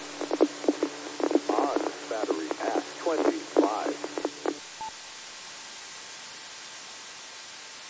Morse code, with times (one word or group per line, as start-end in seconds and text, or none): none